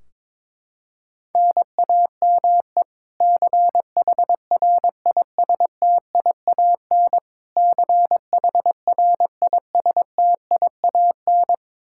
Morse code, heard as a keyed tone, about 22 wpm, 700 Hz. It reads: NAME CHRISTIAN CHRISTIAN
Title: Practice QSO - Exchange 1 - Beginning - 22wpm